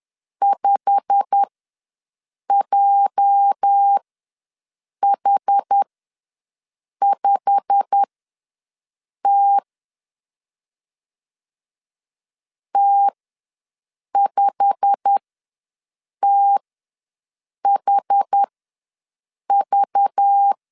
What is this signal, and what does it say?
Morse code, heard as a keyed tone, about 10 wpm, spaced as for 6 wpm, 800 Hz. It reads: 5JH5T T5THV
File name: LEZIONI / 8 YJ45 / 8 YJ45 ws 10wpm 3843